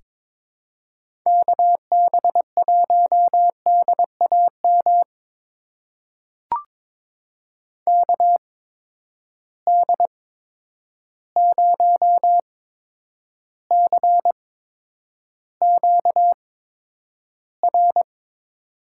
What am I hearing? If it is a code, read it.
Morse code, heard as a keyed tone, about 22 wpm, spaced as for 9 wpm, 700 Hz. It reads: KB1DAM KD0CQR
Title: USA Call Signs - 6 Characters - 8x Character Spacing - 22wpm